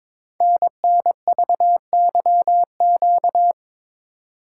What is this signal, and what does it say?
Morse code, heard as a keyed tone, about 22 wpm, 690 Hz. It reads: NNVYQ
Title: Code Groups - 5 Letters - 22wpm